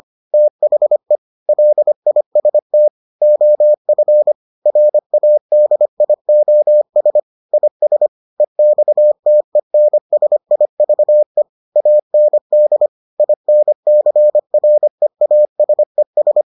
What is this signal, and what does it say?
Morse code, heard as a keyed tone, about 25 wpm, 600 Hz. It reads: THE LIST OF RADIOS IS EXTENSIVE AND INCREASES